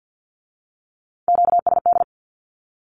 Morse code, heard as a keyed tone, about 50 wpm, 700 Hz. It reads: QSD
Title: Q-Codes with Spoken Meaning - 50wpm